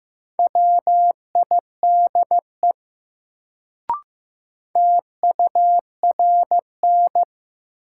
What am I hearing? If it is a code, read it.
Morse code, heard as a keyed tone, about 15 wpm, 700 Hz. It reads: WIDE TURN
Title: Sets of 4 Words - New Word Drill - 15wpm